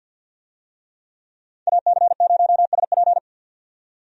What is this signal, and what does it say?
Morse code, heard as a keyed tone, about 50 wpm, 700 Hz. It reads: AC0SP
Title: USA Call Signs - 5 Characters - 50wpm